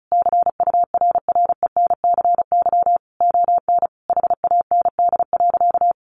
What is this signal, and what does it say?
Morse code, heard as a keyed tone, about 34 wpm, 700 Hz. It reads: CURRENCY ON HAND.
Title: Sentences from Top 1500 Words 35wpm